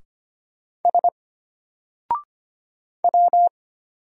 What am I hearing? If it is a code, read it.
Morse code, heard as a keyed tone, about 25 wpm, 700 Hz. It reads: S W
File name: Single-Letter-Number-25wpm